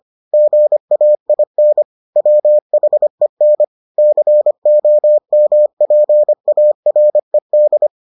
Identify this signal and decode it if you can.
Morse code, heard as a keyed tone, about 25 wpm, 600 Hz. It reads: GAIN WHEN COMPARED